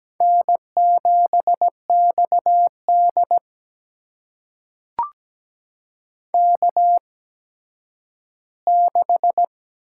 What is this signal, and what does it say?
Morse code, heard as a keyed tone, about 17 wpm, spaced as for 10 wpm, 700 Hz. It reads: N7XD K6
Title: USA Call Signs - 4 Characters - 8x Character Spacing - 17wpm